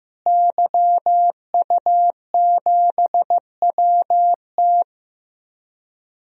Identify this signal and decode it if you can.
Morse code, heard as a keyed tone, about 15 wpm, 700 Hz. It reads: YU7WT
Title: DXCC-100 Call Signs - 5 Characters - Repeated 2x - 15wpm